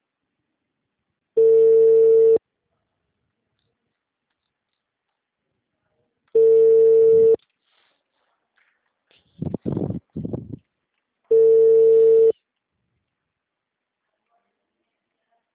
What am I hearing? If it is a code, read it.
Ringback tone (China, 450 Hz).
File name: out- -2006- - - .451